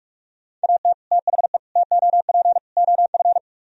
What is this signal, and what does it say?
Morse code, heard as a keyed tone, about 44 wpm, 700 Hz. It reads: AT THE TOP OF